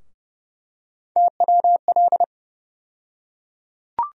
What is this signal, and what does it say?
Morse code, heard as a keyed tone, about 30 wpm, 700 Hz. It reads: TWL
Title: USA Call Sign Postfixes - 3 Letters 30wpm